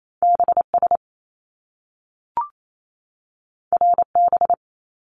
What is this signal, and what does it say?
Morse code, heard as a keyed tone, about 28 wpm, 700 Hz. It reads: BS RB
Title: Top Two Letter Combinations in Words 28wpm